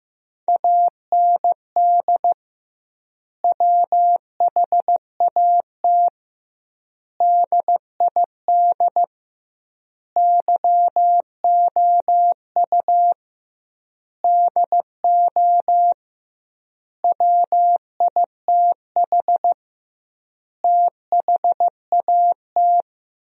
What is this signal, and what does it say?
Morse code, heard as a keyed tone, about 14 wpm, 700 Hz. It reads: AND WHAT DID YOU DO WITH THAT
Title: 2x Word-Spacing - Sentences from Top 300 Words - 15wpm